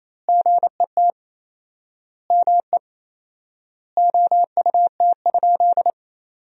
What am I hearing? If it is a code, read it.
Morse code, heard as a keyed tone, about 28 wpm, 700 Hz. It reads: GET ME OUT?
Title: 4x Word-Spacing - Sentences from Top 200 Words - 28wpm